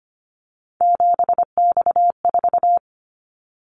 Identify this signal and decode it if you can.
Morse code, heard as a keyed tone, about 25 wpm, 700 Hz. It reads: 7X4
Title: Callsign Prefixes for Top 100 DXCC - 25wpm